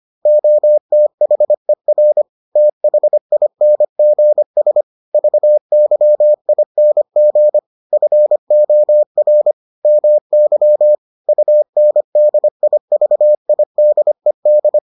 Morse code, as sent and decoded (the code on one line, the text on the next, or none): --- - .... .-. / - .... .. -. --. ... / ...- -.-- .. -. --. / ..-. --- .-. / -- -.-- / ..- -. -.. .. ...- .. -.. -..
OTHER THINGS VYING FOR MY UNDIVIDED